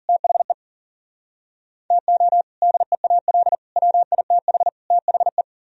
Morse code, heard as a keyed tone, about 51 wpm, 700 Hz. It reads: THE TO DEAL WITH THE